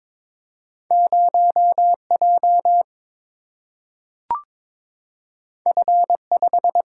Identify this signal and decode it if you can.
Morse code, heard as a keyed tone, about 21 wpm, 700 Hz. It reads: 0J F5
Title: Single-Letter Single-Number 22wpm